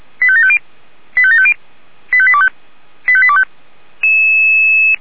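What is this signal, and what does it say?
Five-tone selective call (ZVEI), heard as quick runs of five tones, tones of about 70 ms. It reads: 85750 85750 85825 85825
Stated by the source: ZVEI five-tone selective call received by SDR.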